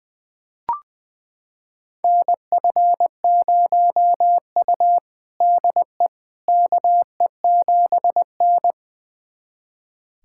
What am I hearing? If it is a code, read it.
Morse code, heard as a keyed tone, about 20 wpm, 700 Hz. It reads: NF0U DE KE7N
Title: USA Callsign DE Callsign - 4 Characters - 20wpm